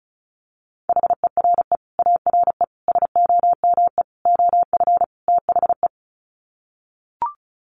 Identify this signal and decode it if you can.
Morse code, heard as a keyed tone, about 34 wpm, 700 Hz. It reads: HERE ARE SOME OF THE
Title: Sets of 5 Words Encore 35wpm